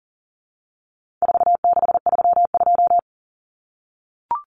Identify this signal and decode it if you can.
Morse code, heard as a keyed tone, about 40 wpm, 700 Hz. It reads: 4632
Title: Common Cut Numbers - 4-Digit 40wpm